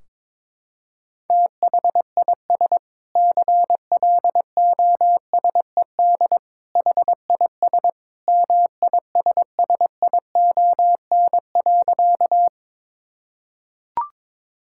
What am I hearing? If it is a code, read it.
Morse code, heard as a keyed tone, about 22 wpm, 700 Hz. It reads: THIS CLOSED HIS MISSION.
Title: Sentences from Top 1600 Words 22wpm